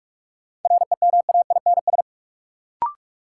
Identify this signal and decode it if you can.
Morse code, heard as a keyed tone, about 42 wpm, 700 Hz. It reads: REMAINS
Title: Sets of 5 Words Encore - New Word Drill - 45wpm